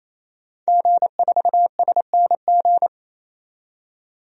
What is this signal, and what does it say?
Morse code, heard as a keyed tone, about 27 wpm, 700 Hz. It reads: G4SNG
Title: DXCC-100 Call Signs - 5 Characters - 28wpm